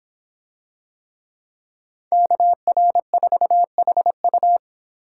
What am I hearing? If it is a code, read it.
Morse code, heard as a keyed tone, about 26 wpm, 700 Hz. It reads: KR4HU